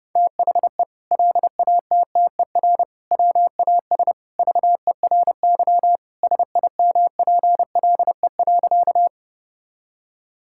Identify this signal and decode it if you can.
Morse code, heard as a keyed tone, about 30 wpm, 700 Hz. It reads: THE LATTER WAS VERY SIMPLE.
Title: Sentences from Top 1600 Words 30wpm